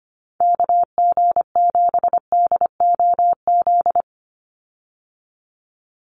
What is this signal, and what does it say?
Morse code, heard as a keyed tone, about 25 wpm, 700 Hz. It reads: KG7DOZ
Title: USA Call Signs 25wpm